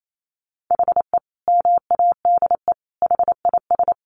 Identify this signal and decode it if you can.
Morse code, heard as a keyed tone, about 28 wpm, 700 Hz. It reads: HE MADE HIS
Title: Sentences from Top 1500 Words 28wpm